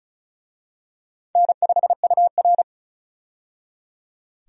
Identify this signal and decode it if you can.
Morse code, heard as a keyed tone, about 34 wpm, 700 Hz. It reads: N5UR
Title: USA Call Signs - 4 Characters - 35wpm